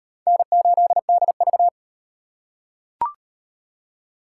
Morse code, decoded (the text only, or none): N8DV